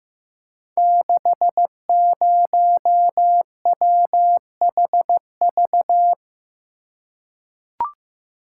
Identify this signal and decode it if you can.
Morse code, heard as a keyed tone, about 15 wpm, 700 Hz. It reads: 60WHV
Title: Code Groups - 5 Letter:Numbers - 15wpm